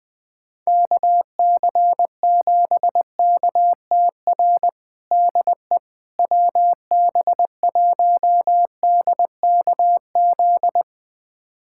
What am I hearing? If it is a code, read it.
Morse code, heard as a keyed tone, about 21 wpm, 700 Hz. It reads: KC7KTR DE WB1DKZ
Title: USA Callsign DE Callsign 20wpm